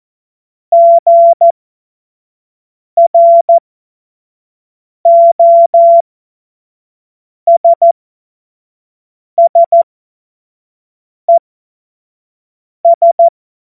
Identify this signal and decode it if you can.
Morse code, heard as a keyed tone, about 14 wpm, spaced as for 5 wpm, 680 Hz. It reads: GROSSES
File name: Morsekurs / Witze / Witz 23 April 2024 25 F70